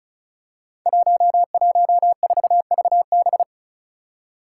Morse code, heard as a keyed tone, about 36 wpm, 700 Hz. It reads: 114VB
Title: Five Challenging Letter-Number Combinations 35wpm